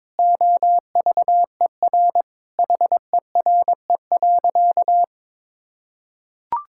Morse code, heard as a keyed tone, about 22 wpm, 700 Hz. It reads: OVER HERE.
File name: practice / 22wpm / Sentences with 4-Word Phrases - 22wpm